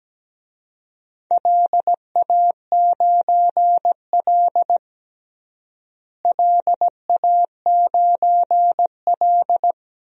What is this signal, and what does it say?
Morse code, heard as a keyed tone, about 17 wpm, 700 Hz. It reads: LA9L LA9L
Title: DXCC-100 Call Signs - 4 Characters - Repeated 3x - 17wpm